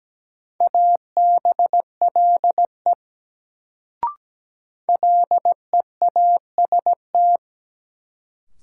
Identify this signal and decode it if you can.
Morse code, heard as a keyed tone, about 17 wpm, 700 Hz. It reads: ABLE LEAST
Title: Sentences from Top 700 Words - New Word Drill - 17wpm